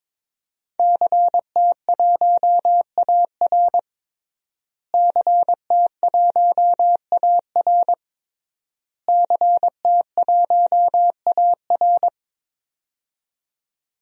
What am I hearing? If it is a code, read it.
Morse code, heard as a keyed tone, about 22 wpm, 700 Hz. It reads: CT1AR CT1AR CT1AR
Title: DXCC-100 Call Signs - 5 Characters - Repeated 3x - 22wpm